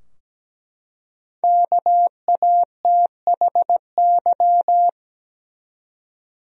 Morse code, heard as a keyed tone, about 17 wpm, 700 Hz. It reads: KATHY
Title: USA Common Names - Max 5 Letters - 17wpm